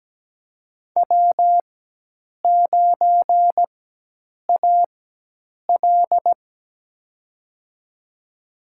Morse code, heard as a keyed tone, about 17 wpm, spaced as for 8 wpm, 700 Hz. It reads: W9AL
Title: USA Call Signs - 4 Characters - 4x Character Spacing - 17wpm